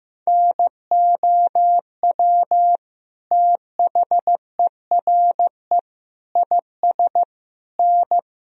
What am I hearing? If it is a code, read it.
Morse code, heard as a keyed tone, about 16 wpm, 700 Hz. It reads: NOW THERE IS N
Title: Sentences from Top 100 Words and Abbreviations 15wpm